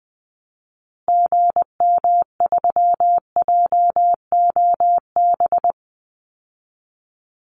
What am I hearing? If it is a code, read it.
Morse code, heard as a keyed tone, about 20 wpm, 700 Hz. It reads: GM3JOB
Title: Call Signs Repeated 2x - 20wpm